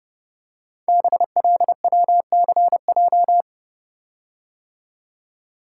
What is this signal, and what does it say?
Morse code, heard as a keyed tone, about 30 wpm, 700 Hz. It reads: BLWCJ
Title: Code Groups - 5 Letters - 30wpm